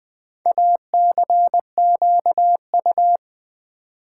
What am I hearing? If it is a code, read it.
Morse code, heard as a keyed tone, about 20 wpm, 700 Hz. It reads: ACQU
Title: Build A Word - 4 Letters At A Time - 11-Letter Words - 20wpm